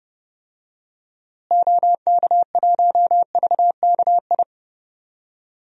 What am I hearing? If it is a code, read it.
Morse code, heard as a keyed tone, about 30 wpm, 700 Hz. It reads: OK1VKI
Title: DXCC-100 Call Signs - 6 Characters - Repeated 2x - 30wpm